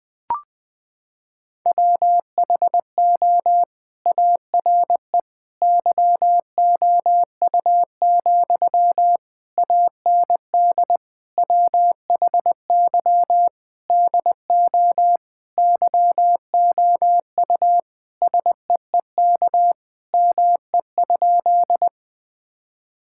Morse code, heard as a keyed tone, about 21 wpm, 700 Hz. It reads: WHO ARE YOU, AND WHY DO YOU SEEK ME?
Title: Sentences from Top 1600 Words 20wpm